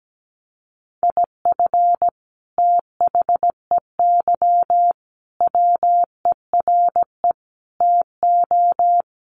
Morse code, heard as a keyed tone, about 17 wpm, 700 Hz. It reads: IF THEY WERE TO
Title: Sentences from Top 1500 Words 17wpm